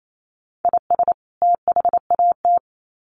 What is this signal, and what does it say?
Morse code, heard as a keyed tone, about 28 wpm, 700 Hz. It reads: IS THAT
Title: Sets of 2 Words 28wpm